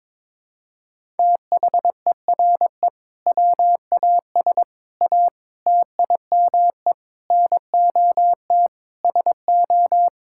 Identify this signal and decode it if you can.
Morse code, heard as a keyed tone, about 22 wpm, 700 Hz. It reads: THERE WAS A TIME NOT SO
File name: practice / 22wpm / Sentences with 8-Word Phrases - 22wpm